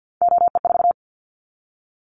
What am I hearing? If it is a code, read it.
Morse code, heard as a keyed tone, about 50 wpm, 700 Hz. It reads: OE4
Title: Callsign Prefixes for Top 100 DXCC - 50wpm